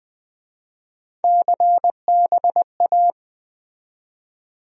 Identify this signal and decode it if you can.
Morse code, heard as a keyed tone, about 22 wpm, 700 Hz. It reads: CBA